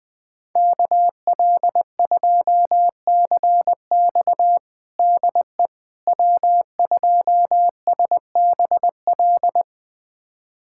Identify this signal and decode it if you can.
Morse code, heard as a keyed tone, about 21 wpm, 700 Hz. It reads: KL2CX DE W2SBL
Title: USA Callsign DE Callsign - 5 Characters - 20wpm